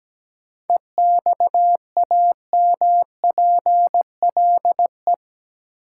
Morse code, heard as a keyed tone, about 17 wpm, 700 Hz. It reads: EXAMPLE
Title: Sets of 2 Words - New Word Drill - 17wpm